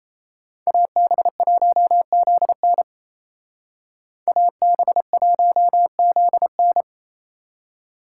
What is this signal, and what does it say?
Morse code, heard as a keyed tone, about 36 wpm, 700 Hz. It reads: AB1ZN AB1ZN